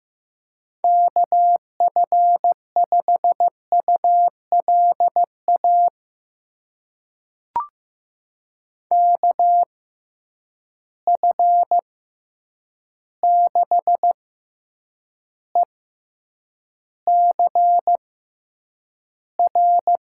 Morse code, heard as a keyed tone, about 15 wpm, spaced as for 8 wpm, 700 Hz. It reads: KF5ULA KF6ECR
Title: USA Call Signs - 6 Characters - 6x Character Spacing - 15wpm